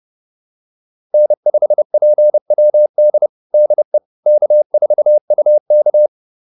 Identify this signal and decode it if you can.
Morse code, heard as a keyed tone, about 30 wpm, 600 Hz. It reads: N5PWD DE K4UK
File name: QSO208 30